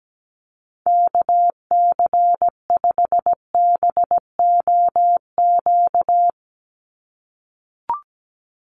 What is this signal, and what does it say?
Morse code, heard as a keyed tone, about 17 wpm, 700 Hz. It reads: KC5BOQ
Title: USA Call Signs 17wpm